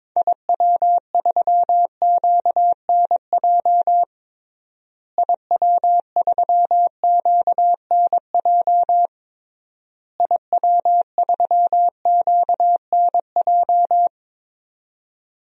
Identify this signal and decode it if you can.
Morse code, heard as a keyed tone, about 21 wpm, 700 Hz. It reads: IW3QNJ IW3QNJ IW3QNJ
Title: DXCC-100 Call Signs - 6 Characters - Repeated 3x - 22wpm